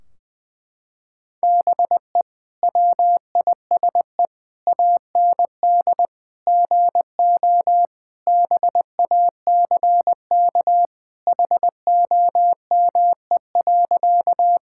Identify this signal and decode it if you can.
Morse code, heard as a keyed tone, about 20 wpm, 700 Hz. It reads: BE WISE AND GO BACK HOME.
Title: Sentences from Top 1600 Words 20wpm